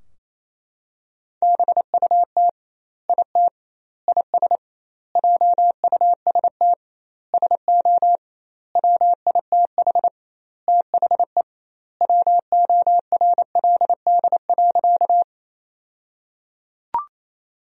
Morse code, heard as a keyed tone, about 27 wpm, 700 Hz. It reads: BUT IT IS JUST SO WITH THE WORLD.
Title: 2x Word-Spacing - Sentences from Top 200 Words - 28wpm